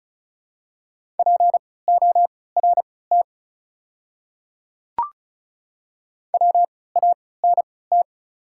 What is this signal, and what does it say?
Morse code, heard as a keyed tone, about 34 wpm, spaced as for 20 wpm, 700 Hz. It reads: PORT WANT
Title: 3x Character Spacing - Top 300 Words - 35wpm